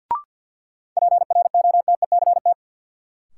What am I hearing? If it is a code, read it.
Morse code, heard as a keyed tone, about 51 wpm, 700 Hz. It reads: PROTECT
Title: Sentences from Top 900 Words - New Word Drill - 50wpm